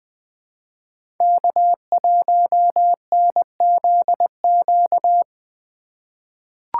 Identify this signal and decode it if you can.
Morse code, heard as a keyed tone, about 21 wpm, 700 Hz. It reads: K1NZQ
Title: USA Call Signs - 5 Characters - 3x Character Spacing - 20wpm